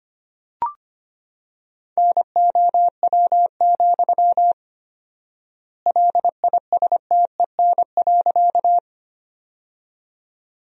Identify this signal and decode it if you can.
Morse code, heard as a keyed tone, about 25 wpm, 700 Hz. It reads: NOW, LISTEN.
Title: 4x Word-Spacing - Sentences from Top 400 Words - 25wpm